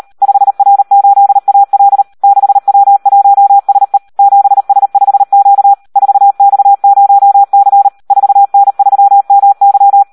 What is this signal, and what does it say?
Morse code, heard as a keyed tone, about 38 wpm, 800 Hz. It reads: 5R8AL 6W1SE 7S5Q 4X0C 4N3MY
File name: Audio practica CW / SD 1 38